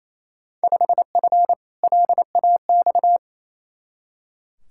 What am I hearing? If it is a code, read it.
Morse code, heard as a keyed tone, about 28 wpm, 700 Hz. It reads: 5F LAX
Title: Field Day Exchanges Encore - 28wpm